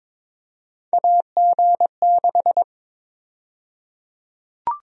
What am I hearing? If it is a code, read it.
Morse code, heard as a keyed tone, about 22 wpm, 700 Hz. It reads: AG6
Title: USA Call Sign Prefixes - 22wpm